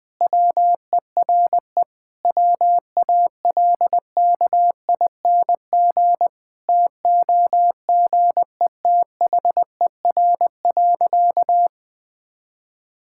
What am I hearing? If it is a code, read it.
Morse code, heard as a keyed tone, about 21 wpm, 700 Hz. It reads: WERE WALKING TOGETHER.
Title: Sentences with 5-Word Phrases - Spoken Beginning - 20wpm